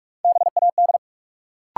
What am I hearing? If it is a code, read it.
Morse code, heard as a keyed tone, about 44 wpm, 700 Hz. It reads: BAD
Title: Sentences from Top 700 Words - New Word Drill - 45wpm